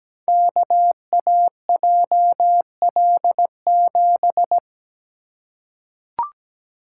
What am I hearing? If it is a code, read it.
Morse code, heard as a keyed tone, about 17 wpm, 690 Hz. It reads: KAJL7
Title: Code Groups - 5 Letter:Numbers - 17wpm